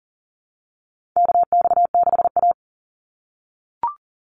Morse code, heard as a keyed tone, about 38 wpm, 700 Hz. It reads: KX6A
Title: NA SOTA Chaser Callsigns 40wpm